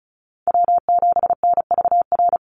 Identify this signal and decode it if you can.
Morse code, heard as a keyed tone, about 34 wpm, 700 Hz. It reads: W7NVR
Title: USA Call Signs 35wpm